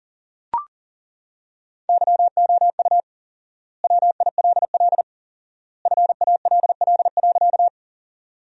Morse code, heard as a keyed tone, about 40 wpm, 700 Hz. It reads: YOU WILL FALL.
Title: 4x Word-Spacing - Sentences from Top 500 Words - 40wpm